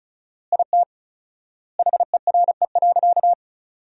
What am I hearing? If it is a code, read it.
Morse code, heard as a keyed tone, about 36 wpm, 700 Hz. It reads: IT HERE.
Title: 4x Word-Spacing - Sentences from Top 500 Words - 35wpm